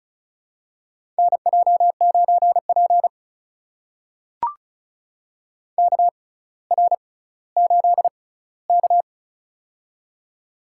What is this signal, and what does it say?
Morse code, heard as a keyed tone, about 36 wpm, spaced as for 18 wpm, 700 Hz. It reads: NJ9P KR8K
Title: USA Call Signs - 4 Characters - 6x Character Spacing - 35wpm